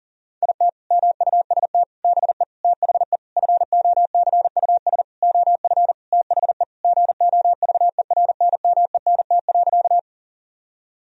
Morse code, heard as a keyed tone, about 40 wpm, 700 Hz. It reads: IT MUST BE THE FOCUS OF THE GOVERNMENT.
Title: Sentences from Top 1000 Words - Review Part 4 - 40wpm